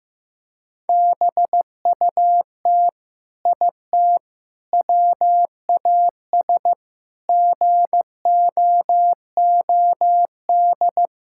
Morse code, heard as a keyed tone, about 15 wpm, 700 Hz. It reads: BUT IT WAS GOOD
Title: Sentences from Top 1000 Words - Review Part 6 - 15wpm